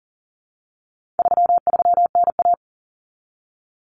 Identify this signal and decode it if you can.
Morse code, heard as a keyed tone, about 40 wpm, 700 Hz. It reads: 33NA
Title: Common Cut Numbers - 4-Digit 40wpm